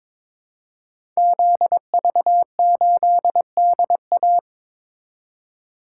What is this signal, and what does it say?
Morse code, heard as a keyed tone, about 22 wpm, 700 Hz. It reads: ZV8DA
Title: Code Groups - 5 Letter:Numbers - 22wpm